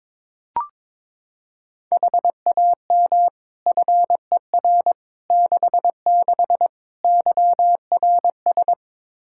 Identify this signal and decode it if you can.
Morse code, heard as a keyed tone, about 22 wpm, 700 Hz. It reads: HAM FER 66 YRS
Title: Practice QSO - Exchange 3 - More Info - 22wpm